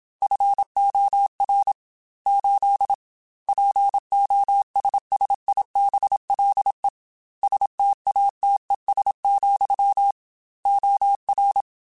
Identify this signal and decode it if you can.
Morse code, heard as a keyed tone, about 27 wpm, 800 Hz. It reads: FOR 8 POSSIBLE STATES, OR
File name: Technical Subjects / Technical Subjects-mp3s / AND Gate @25